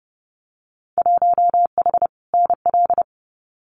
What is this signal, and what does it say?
Morse code, heard as a keyed tone, about 30 wpm, 700 Hz. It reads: 1H NL
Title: Winter Field Day Exchanges 30wpm